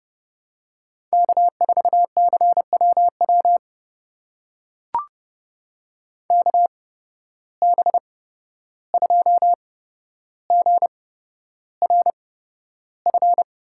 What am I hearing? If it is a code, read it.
Morse code, heard as a keyed tone, about 32 wpm, spaced as for 12 wpm, 700 Hz. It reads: K4CWW KB2GRF
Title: USA Call Signs - 8x Character Spacing - 30wpm